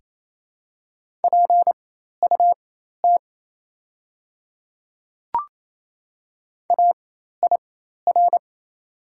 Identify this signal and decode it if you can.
Morse code, heard as a keyed tone, about 28 wpm, spaced as for 13 wpm, 700 Hz. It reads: PUT AIR
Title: 4x Character Spacing - Top 200 Words - 28wpm